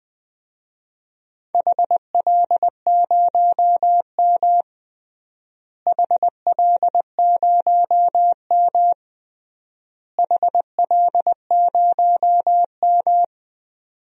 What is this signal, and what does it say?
Morse code, heard as a keyed tone, about 20 wpm, 700 Hz. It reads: HL0M HL0M HL0M
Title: DXCC-100 Call Signs - 4 Characters - Repeated 3x - 20wpm